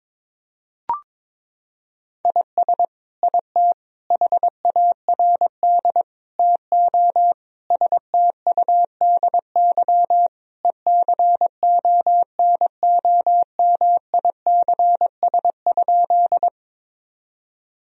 Morse code, heard as a keyed tone, about 22 wpm, 700 Hz. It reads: IS IT HARD TO STUDY ECONOMICS?
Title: Sentences from Top 1000 Words - Review Part 3 - 22wpm